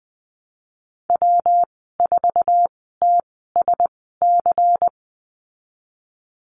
Morse code, heard as a keyed tone, about 20 wpm, spaced as for 15 wpm, 700 Hz. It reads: W4TSC
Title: 128 Lesson 21g - Number 6 - Call Signs 20wpm